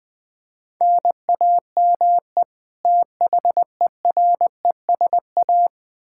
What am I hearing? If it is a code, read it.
Morse code, heard as a keyed tone, about 20 wpm, 700 Hz. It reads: NAME THERESA
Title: K1USN SST QSO Practice 20wpm